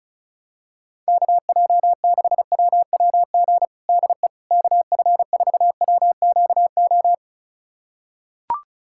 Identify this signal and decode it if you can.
Morse code, heard as a keyed tone, about 36 wpm, 700 Hz. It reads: KJ6WWG DE KF4WQO
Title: USA Callsign DE Callsign - 6 Characters - 35wpm